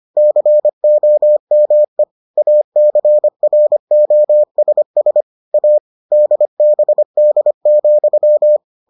Morse code, heard as a keyed tone, about 25 wpm, 600 Hz. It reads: COME ACROSS A DBD,